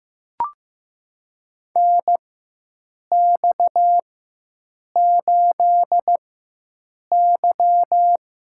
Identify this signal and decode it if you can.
Morse code, heard as a keyed tone, about 15 wpm, spaced as for 7 wpm, 700 Hz. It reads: NX8Y
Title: USA Call Signs - 4 Characters - 4x Character Spacing - 15wpm